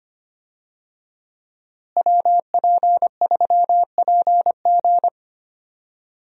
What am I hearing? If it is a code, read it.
Morse code, heard as a keyed tone, about 25 wpm, 700 Hz. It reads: WP3PG
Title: USA Call Signs - 5 Characters - Repeated 3x - 25wpm